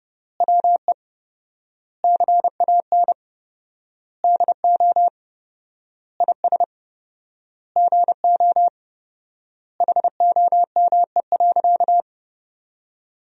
Morse code, heard as a keyed tone, about 32 wpm, 700 Hz. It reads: WE CAN DO IS GO HOME.
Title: 4x Word-Spacing - Sentences from Top 200 Words - 30wpm